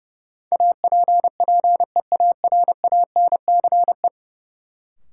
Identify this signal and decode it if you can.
Morse code, heard as a keyed tone, about 31 wpm, 700 Hz. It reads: APPEARANCE
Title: Sentences from Top 1100 Words - New Word Drill - 30wpm